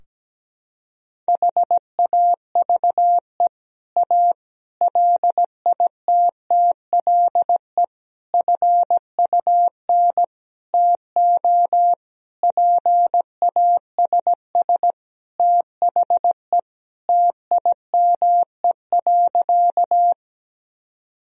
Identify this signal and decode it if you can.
Morse code, heard as a keyed tone, about 17 wpm, 700 Hz. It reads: HAVE A LITTLE FUN TO PASS THE TIME.
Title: Sentences from Top 1000 Words - Review Part 4 - 17wpm